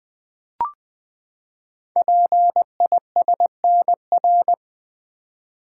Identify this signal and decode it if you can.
Morse code, heard as a keyed tone, about 20 wpm, 700 Hz. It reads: PISNR
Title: Code Groups - 5 Letters - 20wpm